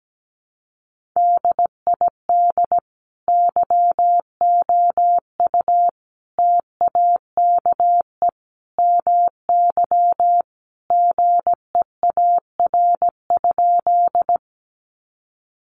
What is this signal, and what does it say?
Morse code, heard as a keyed tone, about 17 wpm, 700 Hz. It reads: DID YOU TAKE MY GEAR?
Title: Sentences from Top 1400 Words 17wpm